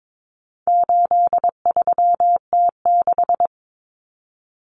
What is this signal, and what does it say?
Morse code, heard as a keyed tone, about 22 wpm, 700 Hz. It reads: 83T6
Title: Common Cut Numbers - 4-Digit 22wpm